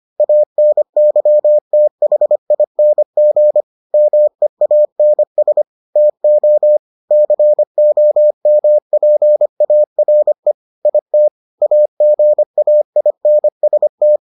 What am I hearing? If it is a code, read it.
Morse code, heard as a keyed tone, about 25 wpm, 600 Hz. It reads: ANYTHING MEANS TO COMPARE IT AGAINST